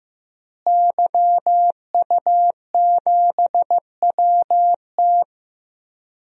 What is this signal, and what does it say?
Morse code, heard as a keyed tone, about 15 wpm, 700 Hz. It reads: YU7WT
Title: DXCC-100 Call Signs - 5 Characters - Repeated 2x - 15wpm